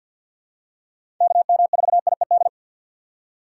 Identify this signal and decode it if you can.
Morse code, heard as a keyed tone, about 47 wpm, 700 Hz. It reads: KM4IED
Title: USA Call Signs - 6 Characters - 4x Character Spacing - 50wpm